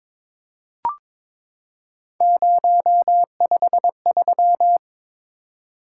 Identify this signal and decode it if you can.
Morse code, heard as a keyed tone, about 22 wpm, 700 Hz. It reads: 053
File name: practice / 22wpm / Three Letter-Number Combinations - 22wpm